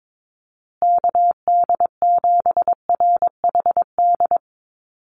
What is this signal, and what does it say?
Morse code, heard as a keyed tone, about 22 wpm, 700 Hz. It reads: KD7RHD
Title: USA Call Signs Repeated 3x 22wpm